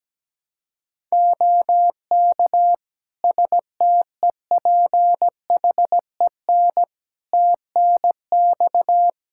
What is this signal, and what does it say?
Morse code, heard as a keyed tone, about 17 wpm, 700 Hz. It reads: OK STEPHEN TNX
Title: Practice QSO - Exchange 4 - Ending - 17wpm